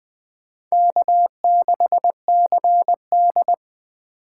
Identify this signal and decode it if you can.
Morse code, heard as a keyed tone, about 23 wpm, 700 Hz. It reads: K6CD
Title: Speed Racing - USA 1x2 Call Signs 20wpm